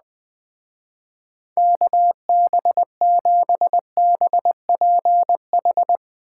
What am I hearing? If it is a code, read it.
Morse code, heard as a keyed tone, about 20 wpm, 700 Hz. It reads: KB7BPH
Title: Speed Racing - USA 2x3 Call Signs 17wpm